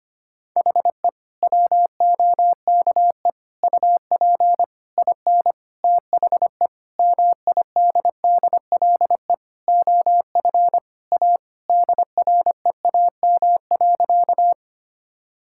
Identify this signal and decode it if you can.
Morse code, heard as a keyed tone, about 25 wpm, 700 Hz. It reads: HE WOKE UP IN THE MIDDLE OF A DREAM.
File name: practice / 25wpm / Sentences with 4-Word Phrases - 25wpm